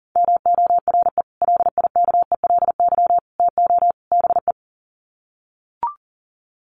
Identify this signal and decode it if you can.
Morse code, heard as a keyed tone, about 40 wpm, 700 Hz. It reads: MORE LIKELY TO BE
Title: Sets of 4 Words 40wpm